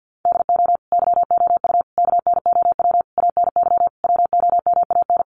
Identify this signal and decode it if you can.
Morse code, heard as a keyed tone, about 50 wpm, 700 Hz. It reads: DO YOU KNOW ANY WOMAN
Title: Sentences from Top 1200 Words 50wpm